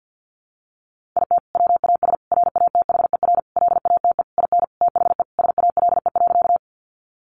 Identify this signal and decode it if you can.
Morse code, heard as a keyed tone, about 50 wpm, 700 Hz. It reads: IT WAS RATHER LATE IN THE SALE.
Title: Sentences from Top 1400 Words 50wpm